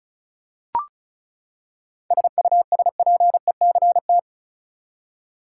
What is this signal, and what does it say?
Morse code, heard as a keyed tone, about 34 wpm, 700 Hz. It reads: SUSPECT
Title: Seven Letter Words - 35wpm